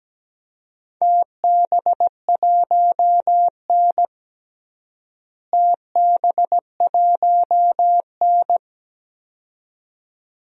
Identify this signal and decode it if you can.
Morse code, heard as a keyed tone, about 17 wpm, 700 Hz. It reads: TB1N TB1N
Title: DXCC-100 Call Signs - 4 Characters - Repeated 2x - 17wpm